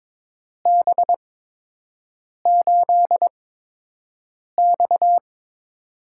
Morse code, heard as a keyed tone, about 22 wpm, 700 Hz. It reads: B 8 X
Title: USA Call Signs - 8x Character Spacing - 22wpm